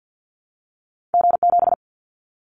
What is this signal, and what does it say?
Morse code, heard as a keyed tone, about 50 wpm, 700 Hz. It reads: G7
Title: Callsign Prefixes for Top 25 DXCC - Repeated 3x - 50wpm